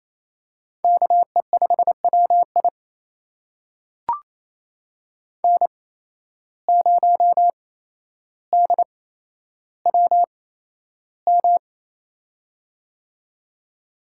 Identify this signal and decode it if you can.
Morse code, heard as a keyed tone, about 28 wpm, spaced as for 12 wpm, 700 Hz. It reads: KE5WI N0DWM
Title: USA Call Signs - 5 Characters - 8x Character Spacing - 28wpm